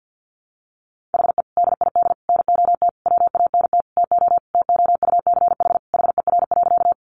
Morse code, heard as a keyed tone, about 50 wpm, 700 Hz. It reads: HE DID NOT WANT TO TOUCH HER.
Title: Sentences from Top 1200 Words 50wpm